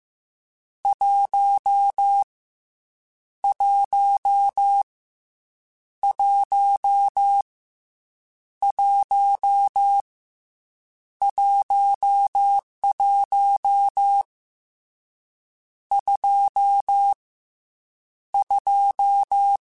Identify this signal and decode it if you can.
Morse code, heard as a keyed tone, about 15 wpm, 800 Hz. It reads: 1 1 1 1 11 2 2